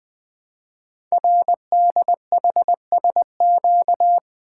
Morse code, heard as a keyed tone, about 20 wpm, 700 Hz. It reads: RDHSQ
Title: Code Groups - 5 Letters - 20wpm